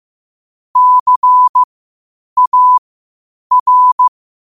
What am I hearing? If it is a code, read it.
Morse code, heard as a keyed tone, about 15 wpm, spaced as for 8 wpm, 1000 Hz. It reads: CAR
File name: qotdfiles / 8wpm